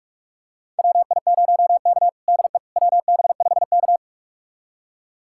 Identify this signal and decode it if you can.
Morse code, heard as a keyed tone, about 41 wpm, 700 Hz. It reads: WI0K DE WB5K
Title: USA Callsign DE Callsign - 4 Characters - 45wpm